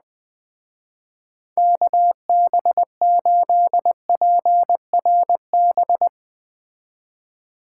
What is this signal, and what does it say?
Morse code, heard as a keyed tone, about 20 wpm, 700 Hz. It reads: KB8PRB
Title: Speed Racing - USA 2x3 Call Signs 17wpm